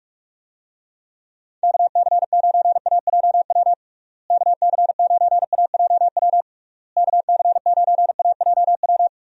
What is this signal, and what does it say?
Morse code, heard as a keyed tone, about 44 wpm, 700 Hz. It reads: KC9AJW KC9AJW KC9AJW
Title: USA Call Signs - 6 Characters - Repeated 3x - 45wpm